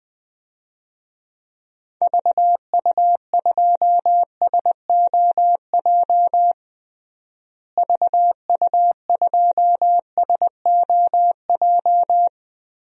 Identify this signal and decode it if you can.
Morse code, heard as a keyed tone, about 20 wpm, 700 Hz. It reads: VU2SOJ VU2SOJ
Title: DXCC-100 Call Signs - 6 Characters - Repeated 2x - 20wpm